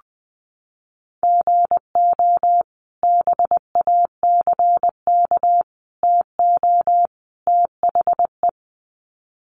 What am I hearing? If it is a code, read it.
Morse code, heard as a keyed tone, about 20 wpm, 700 Hz. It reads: GO BACK TO THE
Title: Sets of 4 Words 20wpm